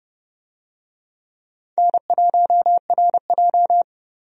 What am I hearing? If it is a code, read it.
Morse code, heard as a keyed tone, about 31 wpm, 700 Hz. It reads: N1RJ
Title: USA Call Signs - 4 Characters - 30wpm